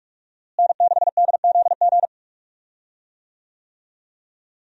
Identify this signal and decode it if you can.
Morse code, heard as a keyed tone, about 42 wpm, 700 Hz. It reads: N6DZG
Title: USA Call Signs - 5 Characters - Repeated 3x - 45wpm